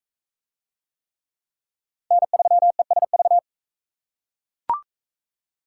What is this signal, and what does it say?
Morse code, heard as a keyed tone, about 37 wpm, 700 Hz. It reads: N3ESV